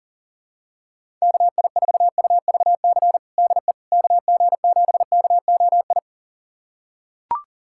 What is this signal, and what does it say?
Morse code, heard as a keyed tone, about 40 wpm, 700 Hz. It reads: KI4UVC DE KG7KOI
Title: USA Callsign DE Callsign - 6 Characters - 40wpm